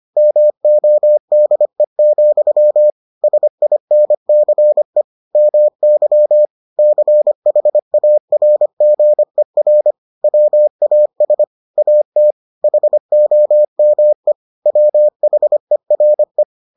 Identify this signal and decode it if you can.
Morse code, heard as a keyed tone, about 25 wpm, 600 Hz. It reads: MODE, SINCE MY CHARGER WAS AT HOME WHERE